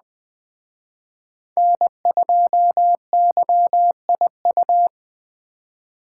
Morse code, heard as a keyed tone, about 20 wpm, 700 Hz. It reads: N2YIU
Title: Speed Racing - USA 1x3 Call Signs 17wpm